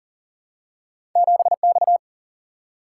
Morse code, heard as a keyed tone, about 40 wpm, 700 Hz. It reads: 7X
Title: Single-Letter Single-Number 40wpm